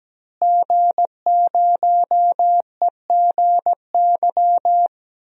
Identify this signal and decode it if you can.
Morse code, heard as a keyed tone, about 17 wpm, 700 Hz. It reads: G0EGY
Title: DXCC-100 Call Signs - 5 Characters - Repeated 3x - 17wpm